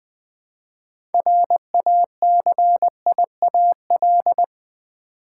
Morse code, heard as a keyed tone, about 20 wpm, 700 Hz. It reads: RACIAL
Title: 6 Letter Words - First 3 Letters Given - 20wpm